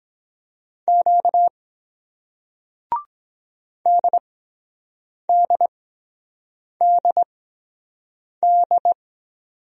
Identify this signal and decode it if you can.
Morse code, heard as a keyed tone, about 24 wpm, spaced as for 7 wpm, 700 Hz. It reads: Q DDDD